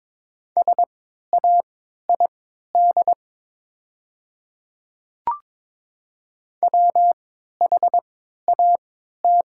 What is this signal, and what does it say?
Morse code, heard as a keyed tone, about 22 wpm, spaced as for 13 wpm, 700 Hz. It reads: SAID WHAT
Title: 3x Character Spacing - Top 100 Words - 22wpm